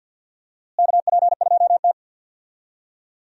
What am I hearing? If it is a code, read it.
Morse code, heard as a keyed tone, about 47 wpm, 700 Hz. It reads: KP2T
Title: DXCC-100 Call Signs - 4 Characters - Repeated 3x - 50wpm